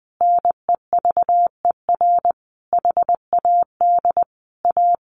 Morse code, heard as a keyed tone, about 20 wpm, 700 Hz. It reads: NEVER HAD A
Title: Sentences from Top 1400 Words 20wpm